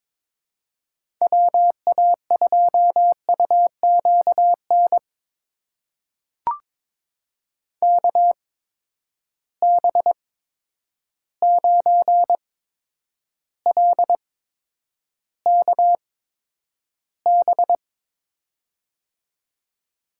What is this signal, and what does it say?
Morse code, heard as a keyed tone, about 22 wpm, spaced as for 9 wpm, 700 Hz. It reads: WA2UQN KB9LKB
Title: USA Call Signs - 6 Characters - 8x Character Spacing - 22wpm